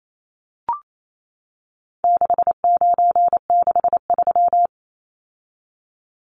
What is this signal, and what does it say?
Morse code, heard as a keyed tone, about 28 wpm, 700 Hz. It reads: 6963
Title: Numbers - 4-Digit 28wpm